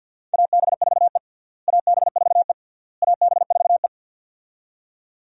Morse code, heard as a keyed tone, about 48 wpm, 700 Hz. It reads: AB4E AB4E AB4E
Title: USA Call Signs - 4 Characters - Repeated 3x - 50wpm